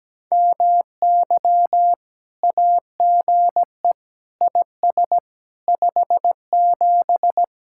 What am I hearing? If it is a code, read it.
Morse code, heard as a keyed tone, about 17 wpm, 700 Hz. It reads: MY AGE IS 57
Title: Speed-Racing Common QSO Phrases - 17wpm